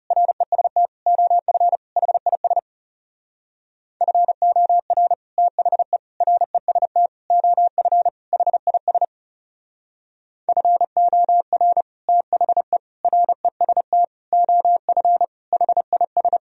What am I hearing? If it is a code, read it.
Morse code, heard as a keyed tone, about 40 wpm, 700 Hz. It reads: REST OF HIS FOR THE REST OF HIS FOR THE REST OF HIS